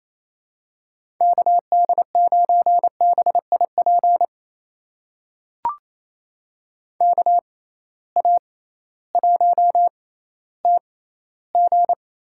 Morse code, heard as a keyed tone, about 28 wpm, spaced as for 15 wpm, 700 Hz. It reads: KD9BIP KA1TG
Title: USA Call Signs - 6 Characters - 6x Character Spacing - 28wpm